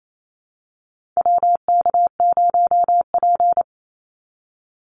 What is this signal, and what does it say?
Morse code, heard as a keyed tone, about 28 wpm, 700 Hz. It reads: WK0P
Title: Call Signs - 28wpm